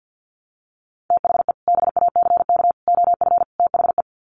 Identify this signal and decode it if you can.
Morse code, heard as a keyed tone, about 48 wpm, 700 Hz. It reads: THE BACK OF THE
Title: Sets of 4 Words 50wpm